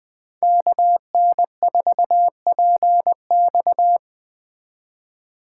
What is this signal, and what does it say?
Morse code, heard as a keyed tone, about 20 wpm, 700 Hz. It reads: KN4PX
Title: Speed Racing - USA Call Signs 17wpm